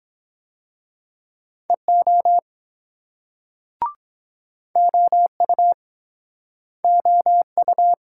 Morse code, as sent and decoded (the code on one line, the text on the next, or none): . --- / --- ..- / --- ..-
EO OU OU